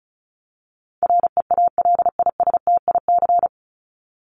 Morse code, heard as a keyed tone, about 34 wpm, 700 Hz. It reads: REALISTIC
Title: Top 1751-2000 Words 35wpm